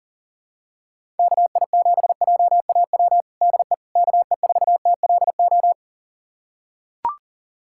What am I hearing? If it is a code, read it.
Morse code, heard as a keyed tone, about 40 wpm, 700 Hz. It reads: KI7JAW DE KE4TLO
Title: USA Callsign DE Callsign 40wpm